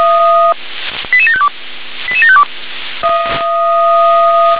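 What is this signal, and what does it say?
Five-tone selective call (ZVEI), heard as quick runs of five tones, tones of about 70 ms. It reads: 88752 88752
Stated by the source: Five-tone ZVEI selective call received by SDR.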